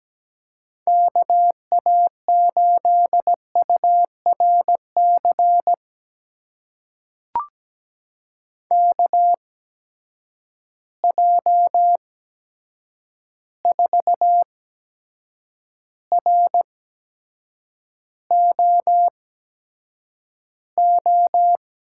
Morse code, heard as a keyed tone, about 17 wpm, spaced as for 7 wpm, 700 Hz. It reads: KA8URC KJ4ROO
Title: USA Call Signs - 6 Characters - 8x Character Spacing - 17wpm